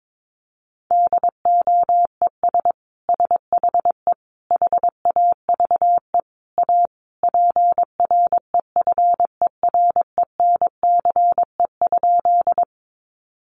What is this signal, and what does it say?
Morse code, heard as a keyed tone, about 22 wpm, 700 Hz. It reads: DOES SHE HAVE A PREFERENCE?
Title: Sentences from Top 1500 Words 22wpm